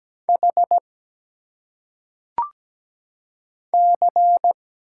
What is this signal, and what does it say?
Morse code, heard as a keyed tone, about 17 wpm, 700 Hz. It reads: H C
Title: Single Letters - Phonetic Pronunciation 17wpm